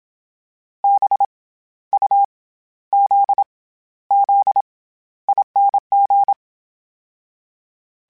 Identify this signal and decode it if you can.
Morse code, heard as a keyed tone, about 27 wpm, 800 Hz. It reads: B U Z Z ING
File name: Special INGs-mp3s / Special INGs-mp3s / SlowToFaster ing @25